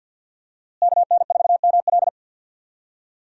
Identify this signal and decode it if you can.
Morse code, heard as a keyed tone, about 53 wpm, 700 Hz. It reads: KN4ML
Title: USA Call Signs - 5 Characters - Repeated 2x - 50wpm